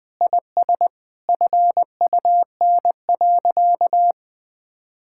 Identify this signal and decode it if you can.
Morse code, heard as a keyed tone, about 20 wpm, 700 Hz. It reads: IS FUN.
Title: Sentences with 2-Word Phrases - 20wpm